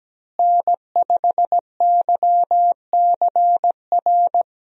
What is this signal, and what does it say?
Morse code, heard as a keyed tone, about 17 wpm, 700 Hz. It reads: N5YCR
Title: USA Call Signs - 5 Characters - Repeated 2x - 17wpm